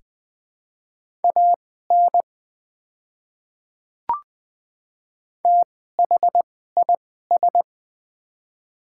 Morse code, heard as a keyed tone, about 20 wpm, spaced as for 14 wpm, 700 Hz. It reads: AN THIS
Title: 2x Character Spacing - Top 300 Words - 20wpm